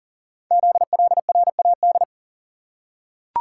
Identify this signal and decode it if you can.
Morse code, heard as a keyed tone, about 38 wpm, 700 Hz. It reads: ZLRAD
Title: Code Groups - 5 Letter:Numbers - 40wpm